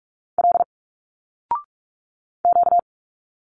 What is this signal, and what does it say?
Morse code, heard as a keyed tone, about 46 wpm, 700 Hz. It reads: L Q
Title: Single Letters 45wpm